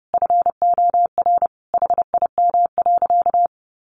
Morse code, heard as a keyed tone, about 30 wpm, 700 Hz. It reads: FOR HIM.
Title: Sentences from Top 500 Words 30wpm